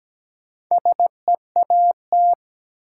Morse code, heard as a keyed tone, about 17 wpm, 700 Hz. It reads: SEAT
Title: Sentences from Top 1500 Words - New Word Drill - 17wpm